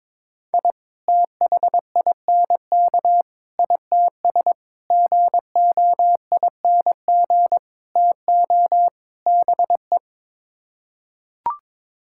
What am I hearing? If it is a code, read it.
Morse code, heard as a keyed tone, about 22 wpm, 700 Hz. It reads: I THINK ITS GOING TO BE
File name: practice / 22wpm / Sets of 6 Words - 22wpm